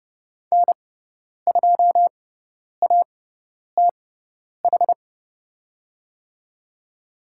Morse code, heard as a keyed tone, about 30 wpm, spaced as for 10 wpm, 700 Hz. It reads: N2ATH
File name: Lesson 18g - Letter m - Call Signs 30wpm